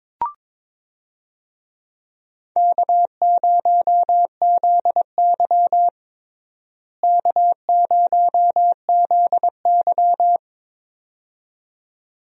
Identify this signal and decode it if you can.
Morse code, heard as a keyed tone, about 21 wpm, 700 Hz. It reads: K0ZY K0ZY
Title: USA Call Signs - 4 Characters - Repeated 2x - 22wpm